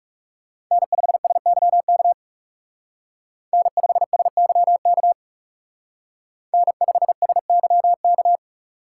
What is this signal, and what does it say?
Morse code, heard as a keyed tone, about 35 wpm, 700 Hz. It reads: N5SYK N5SYK N5SYK
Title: Speed Racing - USA 1x3 Call Signs 30wpm